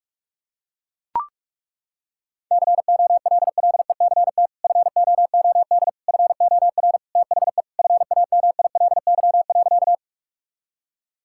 Morse code, heard as a keyed tone, about 41 wpm, 700 Hz. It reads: COLLECT FOOD FOR THE FAMILY.
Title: Sentences from Top 1600 Words 45wpm